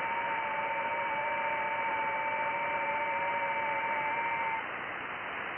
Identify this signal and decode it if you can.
FT8.
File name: IC-7100